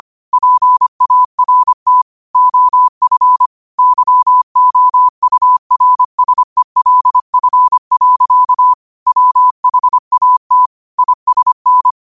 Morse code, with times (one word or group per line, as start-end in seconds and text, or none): 0.3-2.0 PART
2.3-3.5 OF
3.8-8.7 YOURSELF.
9.1-10.7 WHAT
11.0-11.9 ISN